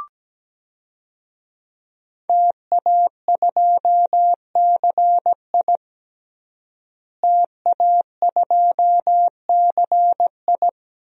Morse code, heard as a keyed tone, about 17 wpm, 700 Hz. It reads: TA2CI TA2CI
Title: DXCC-100 Call Signs - 5 Characters - Repeated 2x - 17wpm